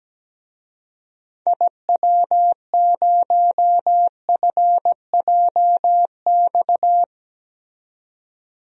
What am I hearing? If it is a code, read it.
Morse code, heard as a keyed tone, about 17 wpm, 700 Hz. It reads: IW0FJX
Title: DXCC-100 Call Signs - 6 Characters - 17wpm